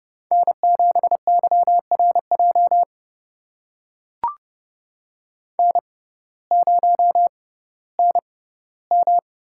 Morse code, heard as a keyed tone, about 30 wpm, spaced as for 16 wpm, 700 Hz. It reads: N7YRJ N0NM
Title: USA Call Signs - 5 Characters - 6x Character Spacing - 30wpm